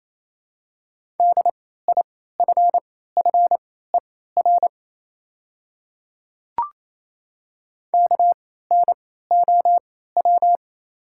Morse code, heard as a keyed tone, about 28 wpm, spaced as for 16 wpm, 700 Hz. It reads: DIFFER KNOW
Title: 3x Character Spacing - Top 300 Words - 28wpm